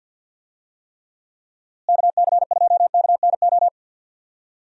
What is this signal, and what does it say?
Morse code, heard as a keyed tone, about 49 wpm, 700 Hz. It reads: KC2KNO
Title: USA Call Signs - 6 Characters - Repeated 2x - 50wpm